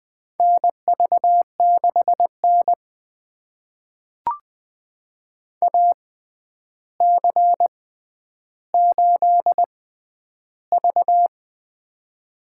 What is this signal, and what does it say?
Morse code, heard as a keyed tone, about 20 wpm, spaced as for 10 wpm, 700 Hz. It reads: NV6N AC8V
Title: USA Call Signs - 4 Characters - 6x Character Spacing - 20wpm